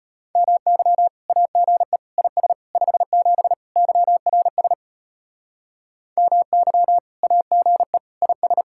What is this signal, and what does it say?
Morse code, heard as a keyed tone, about 32 wpm, 700 Hz. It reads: MY AGE IS 57 YRS MY AGE IS